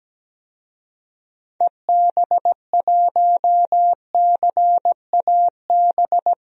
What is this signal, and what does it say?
Morse code, heard as a keyed tone, about 17 wpm, 700 Hz. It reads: EB1CAB
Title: DXCC-100 Call Signs - 6 Characters - 17wpm